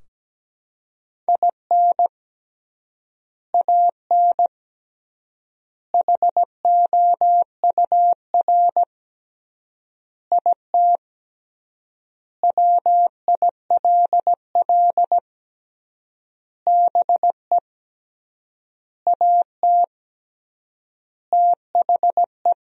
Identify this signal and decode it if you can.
Morse code, heard as a keyed tone, about 17 wpm, 700 Hz. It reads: IN AN HOUR IT WILL BE AT THE
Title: 3x Word-Spacing - Sentences from Top 400 Words - 17wpm